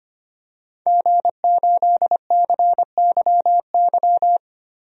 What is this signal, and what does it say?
Morse code, heard as a keyed tone, about 25 wpm, 700 Hz. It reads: G8CYY
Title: DXCC-100 Call Signs - 5 Characters - 25wpm